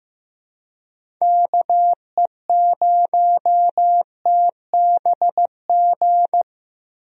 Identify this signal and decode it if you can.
Morse code, heard as a keyed tone, about 15 wpm, 700 Hz. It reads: KE0TBG
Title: USA Call Signs - 6 Characters - 15wpm